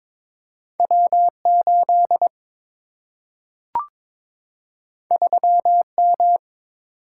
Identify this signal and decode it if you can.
Morse code, heard as a keyed tone, about 22 wpm, 700 Hz. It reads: W 8 3 M
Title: Single-Letter Single-Number 22wpm